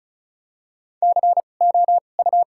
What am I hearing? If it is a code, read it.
Morse code, heard as a keyed tone, about 35 wpm, spaced as for 25 wpm, 700 Hz. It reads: COU
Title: 2x Character Spacing - Top 300 Words - 35wpm